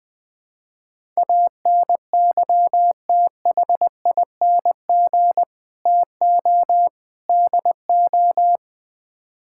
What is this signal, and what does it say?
Morse code, heard as a keyed tone, about 20 wpm, 700 Hz. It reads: ANYTHING TO DO
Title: Build A Phrase - 4-Word Phrases - 20wpm